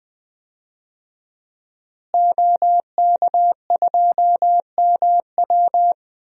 Morse code, heard as a keyed tone, about 20 wpm, 700 Hz. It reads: OK2MW
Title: DXCC-100 Call Signs - 5 Characters - Repeated 3x - 20wpm